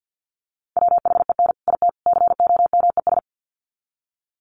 Morse code, heard as a keyed tone, about 50 wpm, 700 Hz. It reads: WHEN IT COMES
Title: Sets of 3 Words 50wpm